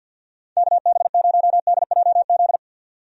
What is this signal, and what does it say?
Morse code, heard as a keyed tone, about 49 wpm, 700 Hz. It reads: KB0DJZ
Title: USA Call Signs - 6 Characters - 50wpm